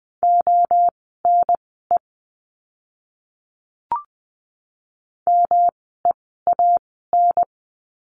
Morse code, heard as a keyed tone, about 20 wpm, spaced as for 14 wpm, 700 Hz. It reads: ONE MEAN
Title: 230 Lesson 37f - Letter x - Words 20wpm